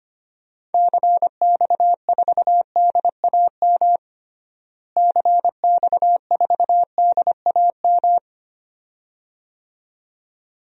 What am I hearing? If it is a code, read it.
Morse code, heard as a keyed tone, about 25 wpm, 700 Hz. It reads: CX4DAM CX4DAM
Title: DXCC-100 Call Signs - 6 Characters - Repeated 2x - 25wpm